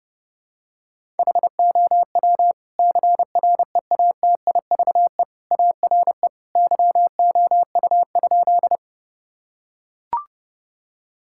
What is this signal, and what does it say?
Morse code, heard as a keyed tone, about 30 wpm, 700 Hz. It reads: HOW CREATIVE ARE YOU?
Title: Sentences from Top 1000 Words - Review Part 5 - 30wpm